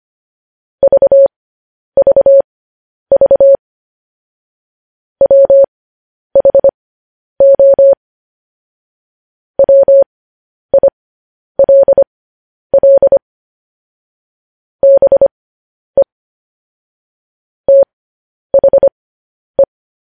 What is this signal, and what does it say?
Morse code, heard as a keyed tone, about 25 wpm, spaced as for 10 wpm, 550 Hz. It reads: VVV WHO WILL BE THE
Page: news-headlines-in-morse-code-at-10-wpm